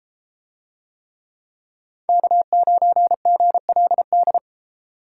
Morse code, heard as a keyed tone, about 32 wpm, 700 Hz. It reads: K9GLD